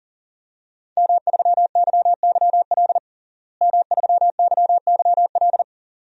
Morse code, heard as a keyed tone, about 40 wpm, 700 Hz. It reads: M3YYL M3YYL
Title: DXCC-100 Call Signs - 5 Characters - Repeated 2x - 40wpm